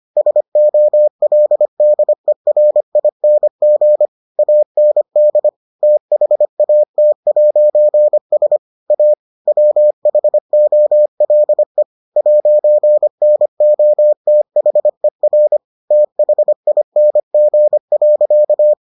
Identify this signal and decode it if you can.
Morse code, heard as a keyed tone, about 25 wpm, 600 Hz. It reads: SOLDERING AND THAT'S A WHOLE 'NOTHER THING.